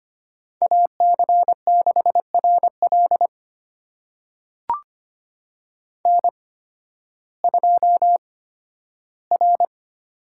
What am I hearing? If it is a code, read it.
Morse code, heard as a keyed tone, about 25 wpm, spaced as for 13 wpm, 700 Hz. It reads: AC6RL N2R
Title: USA Call Signs - 5 Characters - 8x Character Spacing - 25wpm